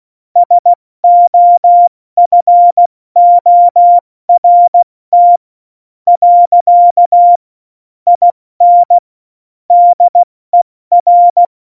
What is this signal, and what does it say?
Morse code, heard as a keyed tone, about 16 wpm, spaced as for 13 wpm, 700 Hz. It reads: SOFORT. IN DER
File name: Morsekurs / Witze / Witz 16. Januar 2018, 70 F80